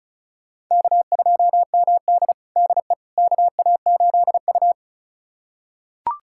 Morse code, heard as a keyed tone, about 36 wpm, 700 Hz. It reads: K2MD DE KA8U
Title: USA Callsign DE Callsign - 4 Characters - 35wpm